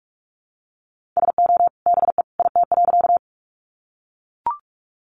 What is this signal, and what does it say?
Morse code, heard as a keyed tone, about 44 wpm, 700 Hz. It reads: SO BE IT.
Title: Sentences from Top 100 Words 45wpm